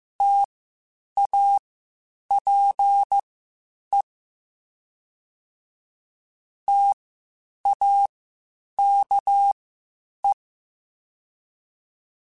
Morse code, heard as a keyed tone, about 15 wpm, spaced as for 8 wpm, 800 Hz. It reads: TAPE TAKE